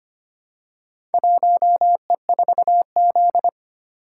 Morse code, heard as a keyed tone, about 25 wpm, 700 Hz. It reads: 1E4Z
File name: practice / 25wpm / Four Letter-Number Combinations - 25wpm